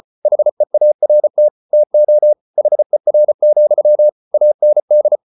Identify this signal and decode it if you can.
Morse code, heard as a keyed tone, about 34 wpm, 600 Hz. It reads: HEART TO HER, AND